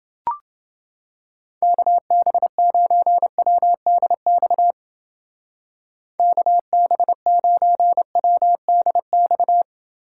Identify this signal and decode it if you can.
Morse code, heard as a keyed tone, about 31 wpm, 700 Hz. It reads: KB9WDX KB9WDX